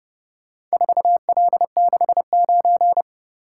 Morse code, heard as a keyed tone, about 29 wpm, 700 Hz. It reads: 4L69